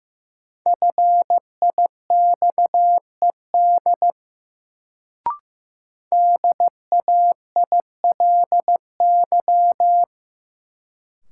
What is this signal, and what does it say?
Morse code, heard as a keyed tone, about 15 wpm, 700 Hz. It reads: FIXED DAILY
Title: Sentences from Top 1100 Words - New Word Drill - 15wpm